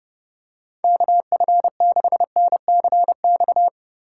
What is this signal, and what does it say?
Morse code, heard as a keyed tone, about 29 wpm, 700 Hz. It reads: KF6NCX
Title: K1USN SST QSO Practice - 30wpm